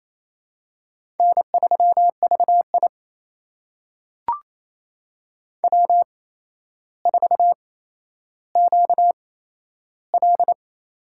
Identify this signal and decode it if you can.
Morse code, heard as a keyed tone, about 28 wpm, spaced as for 12 wpm, 700 Hz. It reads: N3VI W4QL
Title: USA Call Signs - 4 Characters - 8x Character Spacing - 28wpm